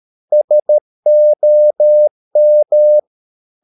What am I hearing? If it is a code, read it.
Morse code, heard as a keyed tone, about 13 wpm, 600 Hz. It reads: SOM
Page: Ferie 13 wpm